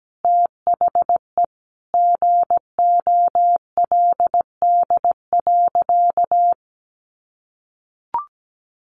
Morse code, heard as a keyed tone, about 17 wpm, 700 Hz. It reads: THE GOLD.
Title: Sentences from Top 1200 Words 17wpm